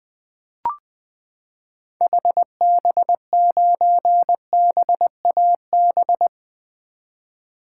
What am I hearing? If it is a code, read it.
Morse code, heard as a keyed tone, about 20 wpm, 700 Hz. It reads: HB9BAB
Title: 2023 SOTA Activator Call Signs - 20wpm